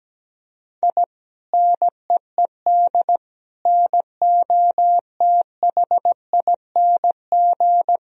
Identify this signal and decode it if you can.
Morse code, heard as a keyed tone, about 17 wpm, 700 Hz. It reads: I NEED NOTHING
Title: Sentences from Top 1000 Words - Review Part 1 - 17wpm